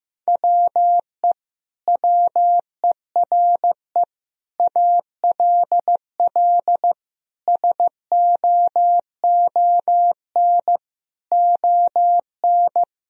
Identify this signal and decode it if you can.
Morse code, heard as a keyed tone, about 15 wpm, 700 Hz. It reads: WE WERE ALL SOON ON
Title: Sentences from Top 1000 Words - Review Part 4 - 15wpm